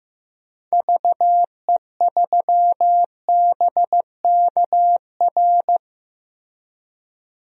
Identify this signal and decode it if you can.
Morse code, heard as a keyed tone, about 15 wpm, 700 Hz. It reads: VE3BKR
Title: DXCC-100 Call Signs - 6 Characters - 15wpm